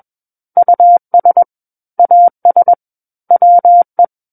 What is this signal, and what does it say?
Morse code, heard as a keyed tone, about 21 wpm, 700 Hz. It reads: US AS WE
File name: Book One / 12-Chapter 12 - 21 WPM